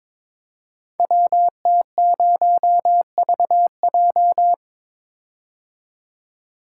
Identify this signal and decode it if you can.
Morse code, heard as a keyed tone, about 22 wpm, 700 Hz. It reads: WT0VJ
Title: Code Groups - 5 Letter:Numbers - 22wpm